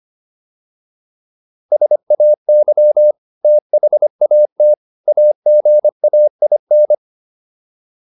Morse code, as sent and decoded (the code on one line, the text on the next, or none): ... .- -.-- / - .... .- - / .- --. .- .. -.
SAY THAT AGAIN